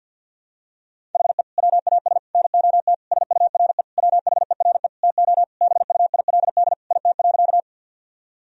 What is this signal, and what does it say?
Morse code, heard as a keyed tone, about 52 wpm, 700 Hz. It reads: HE WAS NOT SURE WHERE TO BUILD IT.
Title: Sentences from Top 1000 Words - Review Part 1 - 50wpm